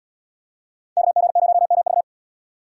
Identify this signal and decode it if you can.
Morse code, heard as a keyed tone, about 74 wpm, 700 Hz. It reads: KG0MV